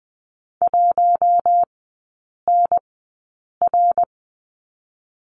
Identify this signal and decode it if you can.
Morse code, heard as a keyed tone, about 20 wpm, spaced as for 8 wpm, 690 Hz. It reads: 1NR
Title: 060 Lesson 11e - Letter h - Three Characters 20wpm